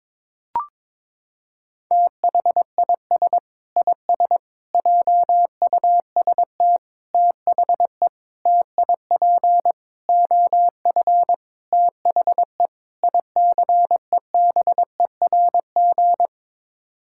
Morse code, heard as a keyed tone, about 22 wpm, 700 Hz. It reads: THIS IS JUST THE TIP OF THE ICEBERG
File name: practice / 22wpm / Sets of 8 Words - 22wpm